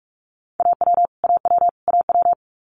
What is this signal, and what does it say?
Morse code, heard as a keyed tone, about 46 wpm, 690 Hz. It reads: AW AW AW
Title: Multi-letter Phonemes Repeated 3x 45wpm